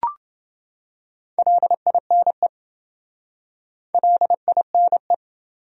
Morse code, heard as a keyed tone, about 28 wpm, 700 Hz. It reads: LINE LINE